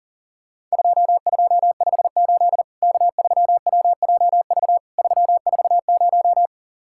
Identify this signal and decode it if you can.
Morse code, heard as a keyed tone, about 40 wpm, 700 Hz. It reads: 2258 K3WJV 340
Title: Worked All Europe DX Contest - QTC Exchange - 40wpm